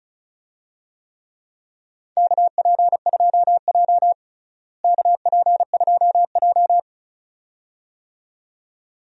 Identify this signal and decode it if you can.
Morse code, heard as a keyed tone, about 35 wpm, 700 Hz. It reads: KP2J KP2J
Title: DXCC-100 Call Signs - 4 Characters - Repeated 2x - 35wpm